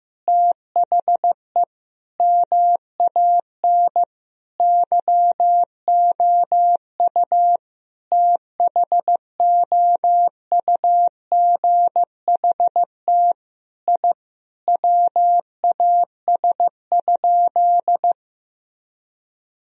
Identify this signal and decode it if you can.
Morse code, heard as a keyed tone, about 15 wpm, 700 Hz. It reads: THE MAN YOU THOUGHT I WAS?
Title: Sentences from Top 1000 Words - Review Part 3 - 15wpm